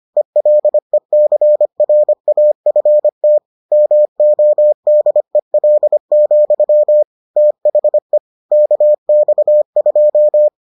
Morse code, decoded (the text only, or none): ELECRAFT MODEL, THE KX2